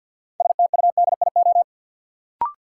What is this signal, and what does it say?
Morse code, heard as a keyed tone, about 53 wpm, 700 Hz. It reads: STUDIO
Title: Sentences from Top 1200 Words - New Word Drill Alt - 50wpm